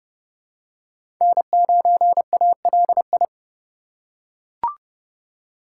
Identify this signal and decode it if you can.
Morse code, heard as a keyed tone, about 30 wpm, 700 Hz. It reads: N9ALI
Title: USA Call Signs - 5 Characters - 2x Character Spacing - 30wpm